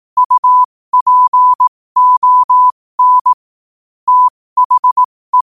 Morse code, heard as a keyed tone, about 18 wpm, spaced as for 15 wpm, 1000 Hz. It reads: UPON THE
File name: qotdfiles / 15wpm